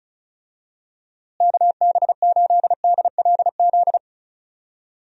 Morse code, heard as a keyed tone, about 36 wpm, 700 Hz. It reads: KB8DLZ
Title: USA Call Signs - 6 Characters - Repeated 3x - 35wpm